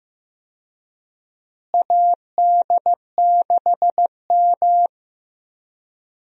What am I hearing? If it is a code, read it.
Morse code, heard as a keyed tone, about 15 wpm, 700 Hz. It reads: AD6M
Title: USA Call Signs - 4 Characters - Repeated 2x - 15wpm